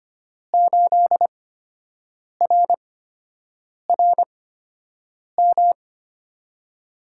Morse code, heard as a keyed tone, about 25 wpm, spaced as for 7 wpm, 700 Hz. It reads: 8RRM
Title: USA Call Signs - 6 Characters - 8x Character Spacing - 25wpm